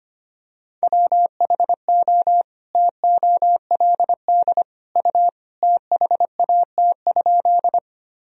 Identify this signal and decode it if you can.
Morse code, heard as a keyed tone, about 25 wpm, 700 Hz. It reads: WHO TOLD U THAT?
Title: Sentences from Top 400 Words and Abbreviations 25wpm